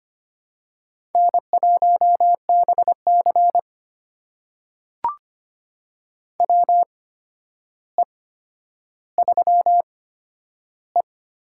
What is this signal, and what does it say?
Morse code, heard as a keyed tone, about 25 wpm, spaced as for 11 wpm, 700 Hz. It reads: N1BC WE3E
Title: USA Call Signs - 4 Characters - 8x Character Spacing - 25wpm